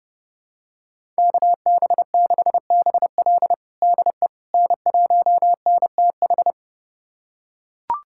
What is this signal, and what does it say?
Morse code, heard as a keyed tone, about 30 wpm, 700 Hz. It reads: KB6BL DE N1NTH
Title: USA Callsign DE Callsign - 5 Characters - 30wpm